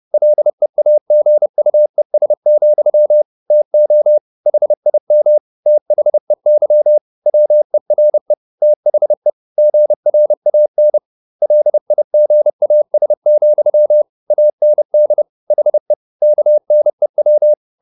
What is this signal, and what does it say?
Morse code, heard as a keyed tone, about 30 wpm, 600 Hz. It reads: LEAGUES, TO HIM THEY WERE THE GRAN LIGAS, AND HE KNEW
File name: the old man and the sea / the old man and the sea 1 30wpm